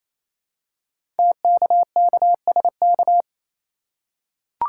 Morse code, heard as a keyed tone, about 27 wpm, 700 Hz. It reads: TKKSK
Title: Code Groups - 5 Letters - 28wpm